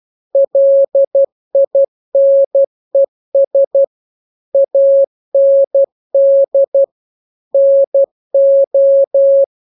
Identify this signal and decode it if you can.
Morse code, heard as a keyed tone, about 12 wpm, 550 Hz. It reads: LINES AND NO